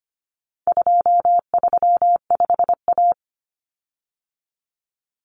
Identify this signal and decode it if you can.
Morse code, heard as a keyed tone, about 25 wpm, 700 Hz. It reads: 235A
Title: Common Cut Numbers - 4-Digit 25wpm